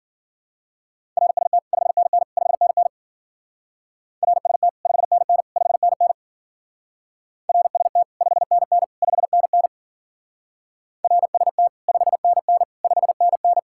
Morse code, heard as a keyed tone, about 51 wpm, 700 Hz. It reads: RST 5NN 5NN RST 5NN 5NN RST 5NN 5NN RST 5NN 5NN